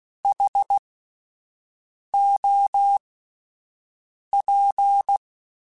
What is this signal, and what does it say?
Morse code, heard as a keyed tone, about 16 wpm, spaced as for 5 wpm, 800 Hz. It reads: HOP
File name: Alphabetical / Alphabetical mp3s / Hs @5